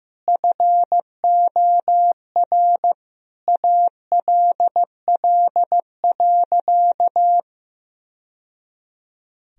Morse code, heard as a keyed tone, about 15 wpm, 700 Hz. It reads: FOR ALL.
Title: Sentences from Top 1000 Words - Review Part 4 - 15wpm